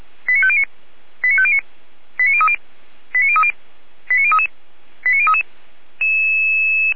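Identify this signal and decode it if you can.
Five-tone selective call (ZVEI), heard as quick runs of five tones, tones of about 70 ms. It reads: 89509 89509 89030 89030 89033 89033
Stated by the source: ZVEI five-tone selective call, address 89010, received on an SDR and logged by Feuerwehr Orlamünde (f85360).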